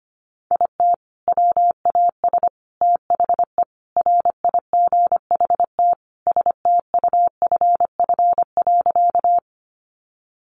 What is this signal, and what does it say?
Morse code, heard as a keyed tone, about 25 wpm, 700 Hz. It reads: IT WAS THE RIGHT STUFF.
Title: Sentences from Top 1400 Words 25wpm